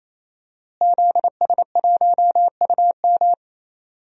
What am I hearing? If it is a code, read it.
Morse code, heard as a keyed tone, about 28 wpm, 700 Hz. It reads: ZS1UM
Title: DXCC-100 Call Signs - 5 Characters - Repeated 3x - 28wpm